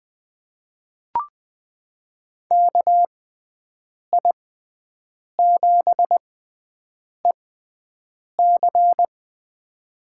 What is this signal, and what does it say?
Morse code, heard as a keyed tone, about 20 wpm, spaced as for 7 wpm, 700 Hz. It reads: KI7EC
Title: USA Call Signs - 6x Character Spacing - 20wpm